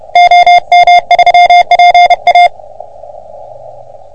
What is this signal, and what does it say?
Morse code, heard as a keyed tone, about 30 wpm, 700 Hz. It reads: OM3PA
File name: OM3PA PILUP CUC 388